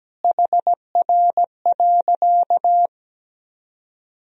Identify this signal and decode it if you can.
Morse code, heard as a keyed tone, about 17 wpm, 700 Hz. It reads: HR.
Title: Sentences from Top 500 Words and Abbreviations 17wpm